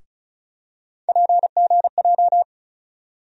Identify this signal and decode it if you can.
Morse code, heard as a keyed tone, about 36 wpm, 700 Hz. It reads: PGJ
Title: USA Call Sign Postfixes - 3 Letters 35wpm